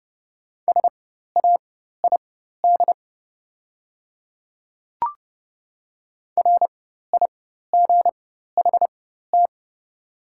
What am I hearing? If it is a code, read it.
Morse code, heard as a keyed tone, about 30 wpm, spaced as for 14 wpm, 700 Hz. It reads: SAID RIGHT
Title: 4x Character Spacing - Top 200 Words - 30wpm